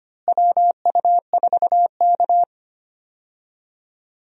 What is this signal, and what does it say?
Morse code, heard as a keyed tone, about 25 wpm, 700 Hz. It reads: WU4K
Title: USA Call Signs - 4 Characters - Repeated 3x - 25wpm